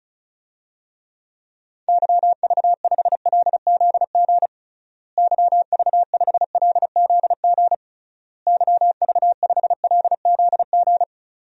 Morse code, heard as a keyed tone, about 34 wpm, 700 Hz. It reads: YV5LZG YV5LZG YV5LZG
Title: DXCC-100 Call Signs - 6 Characters - Repeated 3x - 35wpm